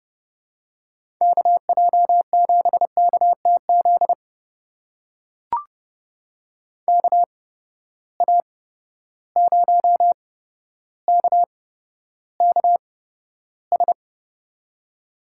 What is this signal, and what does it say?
Morse code, heard as a keyed tone, about 30 wpm, spaced as for 13 wpm, 700 Hz. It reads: KJ7KTZ KA0KKS
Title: USA Call Signs - 8x Character Spacing - 30wpm